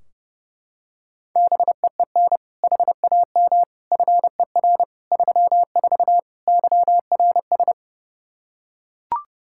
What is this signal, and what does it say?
Morse code, heard as a keyed tone, about 29 wpm, 700 Hz. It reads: BEEN HAM FER 34 YRS
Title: Practice QSO - Exchange 3 - More Info - 30wpm